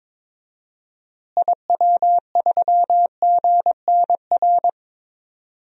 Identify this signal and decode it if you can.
Morse code, heard as a keyed tone, about 22 wpm, 700 Hz. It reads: IW3GNR
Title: DXCC-100 Call Signs - 6 Characters - Repeated 2x - 22wpm